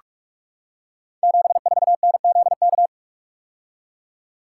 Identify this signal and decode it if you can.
Morse code, heard as a keyed tone, about 44 wpm, 700 Hz. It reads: 74NZK
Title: Code Groups - 5 Letter:Numbers - 45wpm